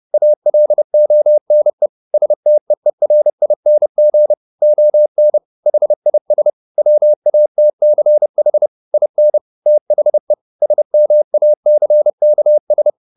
Morse code, heard as a keyed tone, about 30 wpm, 600 Hz. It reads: ALONE STEERING ON HIS WATCH IN THE SMACKS
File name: the old man and the sea / the old man and the sea 1 30wpm